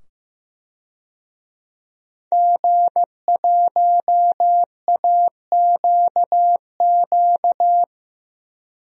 Morse code, heard as a keyed tone, about 15 wpm, 700 Hz. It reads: G1AQQ
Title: DXCC-100 Call Signs - 5 Characters - 15wpm